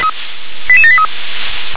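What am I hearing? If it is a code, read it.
Five-tone selective call (ZVEI), heard as quick runs of five tones, tones of about 70 ms. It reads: 88783
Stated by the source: ZVEI five-tone selective call received via SDR.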